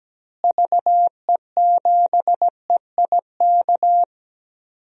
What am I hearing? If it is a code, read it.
Morse code, heard as a keyed tone, about 17 wpm, 700 Hz. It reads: VE7EIK
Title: DXCC-100 Call Signs - 6 Characters - Repeated 3x - 17wpm